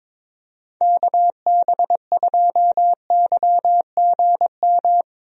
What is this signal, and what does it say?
Morse code, heard as a keyed tone, about 22 wpm, 700 Hz. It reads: KB2YGM
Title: USA Call Signs - 6 Characters - 22wpm